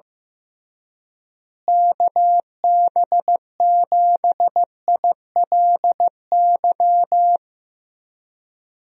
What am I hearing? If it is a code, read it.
Morse code, heard as a keyed tone, about 15 wpm, 700 Hz. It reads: KB7ILY
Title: USA Call Signs - 6 Characters - Repeated 3x - 15wpm